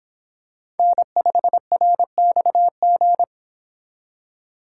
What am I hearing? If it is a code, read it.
Morse code, heard as a keyed tone, about 26 wpm, 700 Hz. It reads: N5RXG